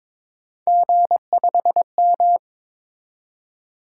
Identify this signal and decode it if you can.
Morse code, heard as a keyed tone, about 27 wpm, 700 Hz. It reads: G5M
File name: Speed-Racing - Three Letter-Number Combinations - 22wpm